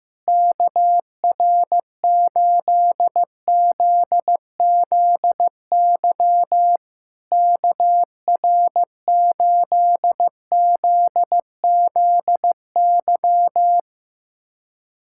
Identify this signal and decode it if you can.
Morse code, heard as a keyed tone, about 15 wpm, 700 Hz. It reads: KR8ZZY KR8ZZY
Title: SOTA QSO Practice 15wpm